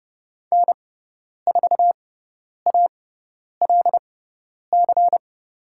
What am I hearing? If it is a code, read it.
Morse code, heard as a keyed tone, about 30 wpm, spaced as for 10 wpm, 700 Hz. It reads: N4ALC
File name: Lesson 17g - Letter u - Call Signs 30wpm